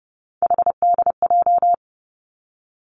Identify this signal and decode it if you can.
Morse code, heard as a keyed tone, about 30 wpm, 700 Hz. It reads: HDJ
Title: Three Challenging Letter-Number Combinations 30wpm